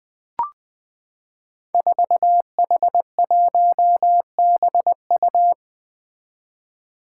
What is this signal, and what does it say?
Morse code, heard as a keyed tone, about 20 wpm, 700 Hz. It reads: 4H1BU
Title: Five Challenging Letter-Number Combinations 20wpm